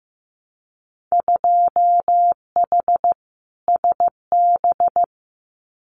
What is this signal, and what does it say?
Morse code, heard as a keyed tone, about 15 wpm, 700 Hz. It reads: 2H SB
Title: Winter Field Day Exchanges Encore 15wpm